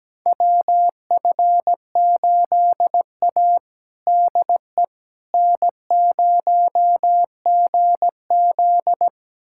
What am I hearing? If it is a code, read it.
Morse code, heard as a keyed tone, about 17 wpm, 700 Hz. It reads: WF8A DE N0GZ
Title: USA Callsign DE Callsign - 4 Characters - 17wpm